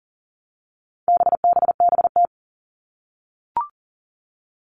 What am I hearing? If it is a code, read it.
Morse code, heard as a keyed tone, about 40 wpm, 700 Hz. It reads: BBBT
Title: All Cut Numbers - 4-Digit 40wpm